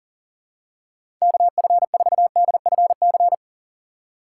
Morse code, heard as a keyed tone, about 40 wpm, 700 Hz. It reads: KF4DFC
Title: USA Call Signs - 6 Characters - 3x Character Spacing - 40wpm